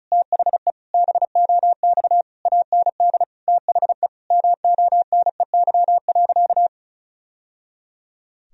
Morse code, heard as a keyed tone, about 34 wpm, 700 Hz. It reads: THE BOX AND THE MONEY.
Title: Sentences from Top 1000 Words - Review Part 2 - 35wpm